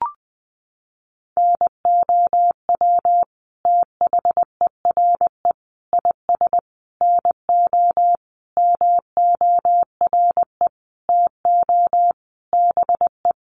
Morse code, heard as a keyed tone, about 20 wpm, 700 Hz. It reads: NOW THERE IS NO MORE TO BE
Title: Sentences from Top 100 Words 20wpm